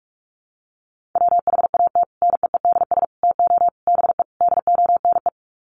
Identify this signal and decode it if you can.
Morse code, heard as a keyed tone, about 44 wpm, 700 Hz. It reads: WHAT NEEDS TO BE DONE
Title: Sets of 5 Words Encore 45wpm